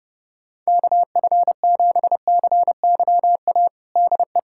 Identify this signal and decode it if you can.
Morse code, heard as a keyed tone, about 30 wpm, 700 Hz. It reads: KF7CYA DE
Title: USA Callsign DE Callsign - 6 Characters - 30wpm